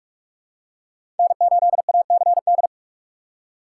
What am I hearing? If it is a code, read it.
Morse code, heard as a keyed tone, about 44 wpm, 700 Hz. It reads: N8ACD
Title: USA Call Signs - 5 Characters - 4x Character Spacing - 45wpm